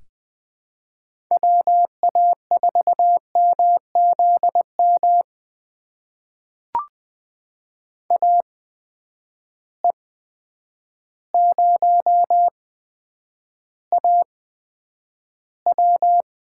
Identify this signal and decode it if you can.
Morse code, heard as a keyed tone, about 20 wpm, spaced as for 9 wpm, 700 Hz. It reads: WA4MZM AE0AW
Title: USA Call Signs - 8x Character Spacing - 20wpm